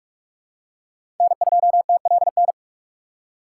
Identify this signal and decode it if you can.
Morse code, heard as a keyed tone, about 46 wpm, 700 Hz. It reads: N2TLN
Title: USA Call Signs - 5 Characters - 2x Character Spacing - 45wpm